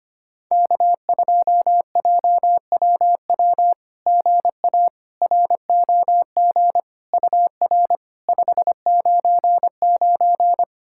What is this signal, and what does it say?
Morse code, heard as a keyed tone, about 25 wpm, 700 Hz. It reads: K2JWW GA ROG UR 599
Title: SOTA QSO Practice 25wpm